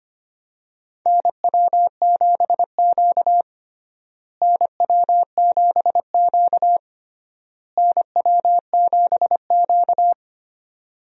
Morse code, heard as a keyed tone, about 25 wpm, 700 Hz. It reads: NW7Q NW7Q NW7Q
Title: USA Call Signs - 4 Characters - Repeated 3x - 25wpm